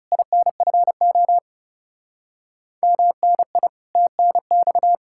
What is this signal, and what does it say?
Morse code, heard as a keyed tone, about 37 wpm, 700 Hz. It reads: INFO MNI TNX
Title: Speed-Racing Common QSO Phrases Encore - 30wpm